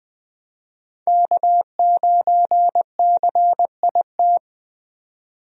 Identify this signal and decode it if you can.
Morse code, heard as a keyed tone, about 20 wpm, 700 Hz. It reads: K9CIT
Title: USA Call Signs - 5 Characters - 20wpm